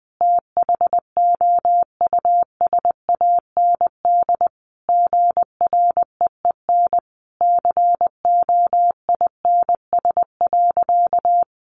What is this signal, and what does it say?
Morse code, heard as a keyed tone, about 20 wpm, 700 Hz. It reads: THOUSAND GREEN COINS.
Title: A Sound Of Thunder 20wpm